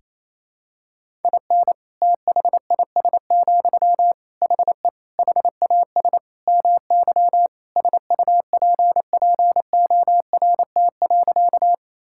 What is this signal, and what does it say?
Morse code, heard as a keyed tone, about 28 wpm, 700 Hz. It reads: IN THIS, HE HAS MY SUPPORT.
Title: Sentences from Top 1000 Words - Review Part 5 - 28wpm